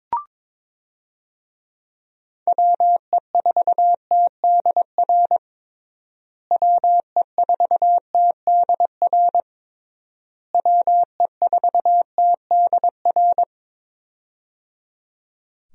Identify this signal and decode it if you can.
Morse code, heard as a keyed tone, about 22 wpm, 700 Hz. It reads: WE4TDR WE4TDR WE4TDR
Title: USA Call Signs - 6 Characters - Repeated 3x - 22wpm